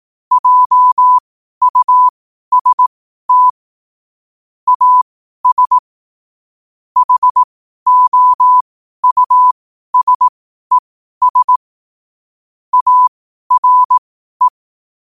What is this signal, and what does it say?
Morse code, heard as a keyed tone, about 18 wpm, spaced as for 12 wpm, 1000 Hz. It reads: JUST AS HOUSES ARE
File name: qotdfiles / 12wpm